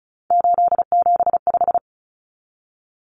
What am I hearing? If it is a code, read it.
Morse code, heard as a keyed tone, about 34 wpm, 700 Hz. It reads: 875
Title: Numbers - 3-Digit 35wpm